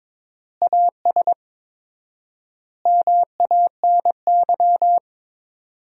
Morse code, heard as a keyed tone, about 22 wpm, 700 Hz. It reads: AS MANY
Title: 4x Word-Spacing - Sentences from Top 400 Words - 22wpm